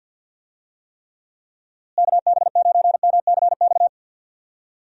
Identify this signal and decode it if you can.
Morse code, heard as a keyed tone, about 52 wpm, 700 Hz. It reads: KB9MCX
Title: USA Call Signs - 6 Characters - Repeated 3x - 50wpm